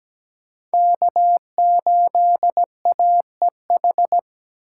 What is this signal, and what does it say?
Morse code, heard as a keyed tone, about 17 wpm, 700 Hz. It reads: K8AEH
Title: USA Call Signs - 5 Characters - 4x Character Spacing - 17wpm